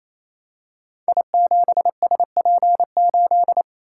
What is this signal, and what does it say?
Morse code, heard as a keyed tone, about 29 wpm, 700 Hz. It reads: I7SP8
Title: Code Groups - 5 Letter:Numbers - 28wpm